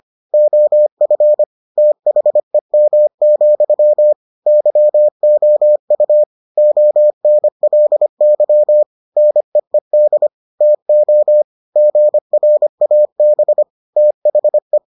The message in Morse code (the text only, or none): OF THEM, YOU ONLY NEED TO GRAB THE